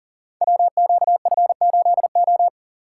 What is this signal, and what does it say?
Morse code, heard as a keyed tone, about 42 wpm, 700 Hz. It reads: WQF8O
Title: Code Groups - 5 Letter:Numbers - 40wpm